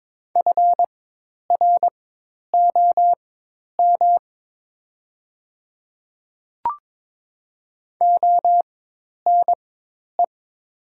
Morse code, heard as a keyed tone, about 22 wpm, spaced as for 10 wpm, 690 Hz. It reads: FROM ONE
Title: 4x Character Spacing - Top 300 Words - 22wpm